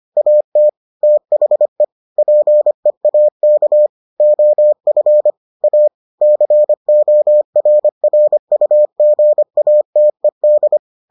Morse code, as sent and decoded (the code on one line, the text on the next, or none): .- - / - .... / .--. .- -.- / --- ..-. / .- / -.-. --- .-. .-. ..- --. .- - . -..
AT THE PEAK OF A CORRUGATED